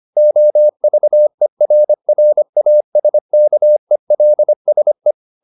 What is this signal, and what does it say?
Morse code, heard as a keyed tone, about 25 wpm, 600 Hz. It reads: OVERRASKELSE
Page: Ferie 25 wpm